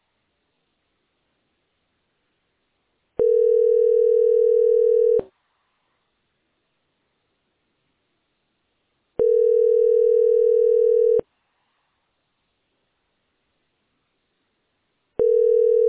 Ringback tone (North America), 440 + 480 Hz.